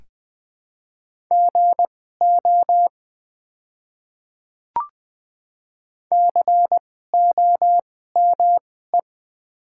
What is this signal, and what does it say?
Morse code, heard as a keyed tone, about 20 wpm, spaced as for 14 wpm, 700 Hz. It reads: GO COME
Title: 2x Character Spacing - Top 200 Words - 20wpm